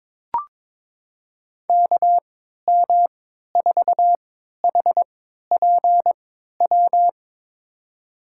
Morse code, heard as a keyed tone, about 22 wpm, spaced as for 13 wpm, 700 Hz. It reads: KM4HPW
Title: USA Call Signs - 6 Characters - 3x Character Spacing - 22wpm